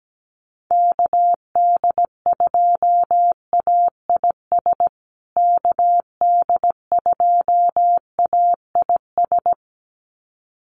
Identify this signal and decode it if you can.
Morse code, heard as a keyed tone, about 17 wpm, 700 Hz. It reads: KD2AIS KD2AIS
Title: USA Call Signs Repeated 2x 17wpm